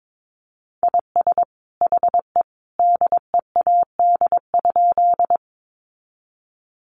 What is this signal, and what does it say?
Morse code, heard as a keyed tone, about 22 wpm, 700 Hz. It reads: IS HE DEAD?
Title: Sentences from Top 1500 Words 22wpm